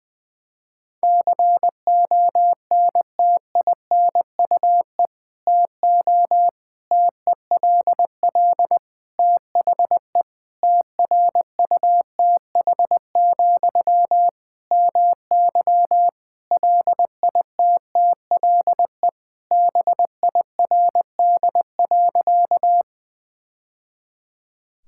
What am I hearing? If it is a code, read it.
Morse code, heard as a keyed tone, about 20 wpm, 700 Hz. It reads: CONTINUE TO TELL THE TRUTH, MY LITTLE BIRD.
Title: Sentences from Top 1000 Words - Review Part 6 - 20wpm